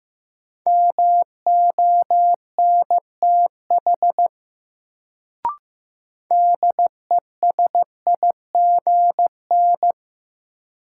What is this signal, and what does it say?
Morse code, heard as a keyed tone, about 15 wpm, 700 Hz. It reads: MONTH DESIGN
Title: Sentences from Top 800 Words - New Word Drill - 15wpm